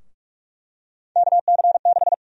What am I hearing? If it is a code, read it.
Morse code, heard as a keyed tone, about 46 wpm, 700 Hz. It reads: KC6
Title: USA Call Sign Prefixes - 45wpm